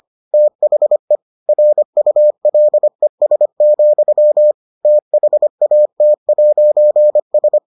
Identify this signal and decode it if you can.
Morse code, heard as a keyed tone, about 25 wpm, 600 Hz. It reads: THE RULES, THAT'S